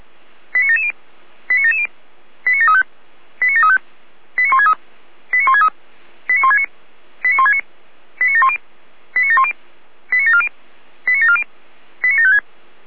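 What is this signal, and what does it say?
Five-tone selective call (ZVEI), heard as quick runs of five tones, tones of about 70 ms. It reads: 89770 89770 89735 89735 89163 89163 89169 89169 89710 89710 89740 89740 89756
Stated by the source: Five-tone ZVEI selective call received on SDR.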